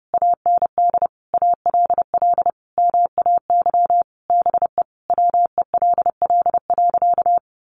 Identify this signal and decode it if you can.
Morse code, heard as a keyed tone, about 30 wpm, 700 Hz. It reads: AND ALL MAY BE WELL.
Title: Sentences from Top 1200 Words 30wpm